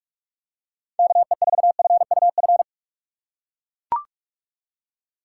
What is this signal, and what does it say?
Morse code, heard as a keyed tone, about 43 wpm, 700 Hz. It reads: KE4FUF